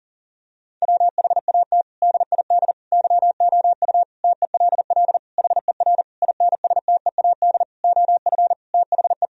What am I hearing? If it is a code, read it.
Morse code, heard as a keyed tone, about 40 wpm, 700 Hz. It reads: WHAT DID YOU TELL HER INSTEAD OF THE
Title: Sentences from Top 1000 Words - Review Part 4 - 40wpm